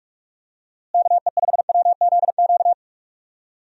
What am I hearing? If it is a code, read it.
Morse code, heard as a keyed tone, about 44 wpm, 700 Hz. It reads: KE5WZQ
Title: USA Call Signs - 6 Characters - Repeated 2x - 45wpm